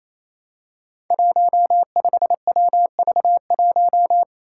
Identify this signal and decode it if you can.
Morse code, heard as a keyed tone, about 27 wpm, 700 Hz. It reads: 15WV1
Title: Five Challenging Letter-Number Combinations 28wpm